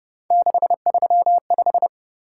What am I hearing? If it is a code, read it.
Morse code, heard as a keyed tone, about 31 wpm, 700 Hz. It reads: 635
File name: Speed-Racing - Three Letter-Number Combinations - 25wpm